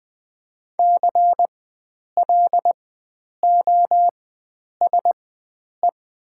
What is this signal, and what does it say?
Morse code, heard as a keyed tone, about 20 wpm, spaced as for 9 wpm, 700 Hz. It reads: CLOSE
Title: 4x Character Spacing - Top 500 Words - 20wpm